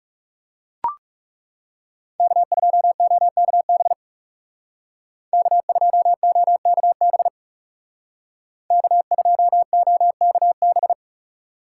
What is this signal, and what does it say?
Morse code, heard as a keyed tone, about 36 wpm, 700 Hz. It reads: K2OKB K2OKB K2OKB
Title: Speed Racing - USA 1x3 Call Signs 30wpm